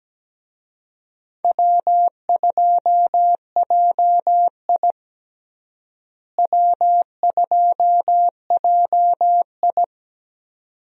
Morse code, heard as a keyed tone, about 17 wpm, 700 Hz. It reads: W2JI W2JI
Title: USA Call Signs - 4 Characters - Repeated 2x - 17wpm